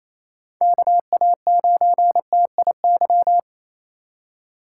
Morse code, heard as a keyed tone, about 36 wpm, 700 Hz. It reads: KA9TIY